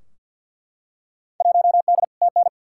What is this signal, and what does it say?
Morse code, heard as a keyed tone, about 50 wpm, 700 Hz. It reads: 1D TN
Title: Field Day Exchanges - 50wpm